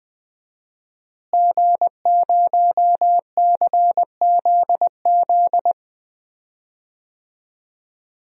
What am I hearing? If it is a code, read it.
Morse code, heard as a keyed tone, about 20 wpm, 700 Hz. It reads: G0CZZ
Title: DXCC-100 Call Signs - 5 Characters - Repeated 3x - 20wpm